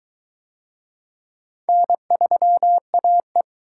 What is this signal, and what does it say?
Morse code, heard as a keyed tone, about 22 wpm, 700 Hz. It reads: N3AE